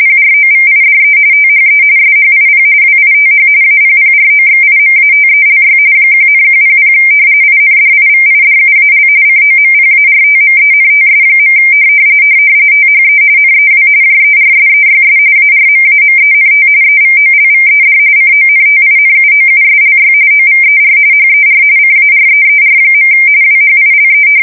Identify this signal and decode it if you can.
RTTY, about 45.45 baud. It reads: PROVIDE THE SHIP WITH APPROXIMATELY 1,100 INDIVIDUAL COMPARTMENTS. THE NEW JERSEYS ARMAMENT EVOLVED CONTINUOUSLY OVER HER CAREER. WHILE
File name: fsk170 / battleshipnj